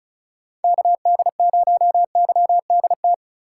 Morse code, heard as a keyed tone, about 34 wpm, 700 Hz. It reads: KD0YDT
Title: USA Call Signs - 6 Characters - 3x Character Spacing - 35wpm